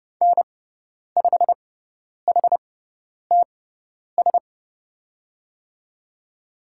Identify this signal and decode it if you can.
Morse code, heard as a keyed tone, about 30 wpm, spaced as for 10 wpm, 700 Hz. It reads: N5HTS
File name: Lesson 16g - Letter c - Call Signs 30wpm